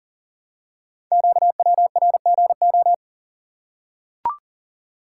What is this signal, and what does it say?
Morse code, heard as a keyed tone, about 38 wpm, 700 Hz. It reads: QWRGO
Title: Code Groups - 5 Letters - 40wpm